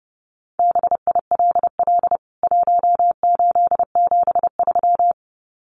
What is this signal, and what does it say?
Morse code, heard as a keyed tone, about 30 wpm, 700 Hz. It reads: BILL 1873
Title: CWT Exchange 30wpm